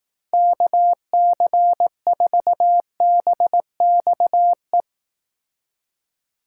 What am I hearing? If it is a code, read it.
Morse code, heard as a keyed tone, about 18 wpm, 700 Hz. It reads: KC4BXE
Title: Speed Racing - USA 2x3 Call Signs 15wpm